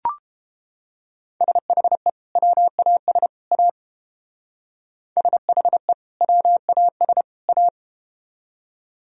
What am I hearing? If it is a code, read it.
Morse code, heard as a keyed tone, about 30 wpm, 700 Hz. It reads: SHE WAS A SHE WAS A